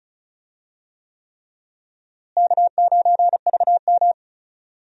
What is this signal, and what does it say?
Morse code, heard as a keyed tone, about 34 wpm, 700 Hz. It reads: K9VM
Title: USA Call Signs - 4 Characters - Repeated 3x - 35wpm